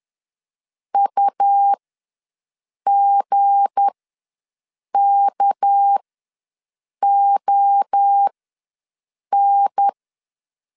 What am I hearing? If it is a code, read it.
Morse code, heard as a keyed tone, about 10 wpm, spaced as for 6 wpm, 800 Hz. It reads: UGKON
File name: LEZIONI / 7 GF9Z / 7 G ws 10wpm 3843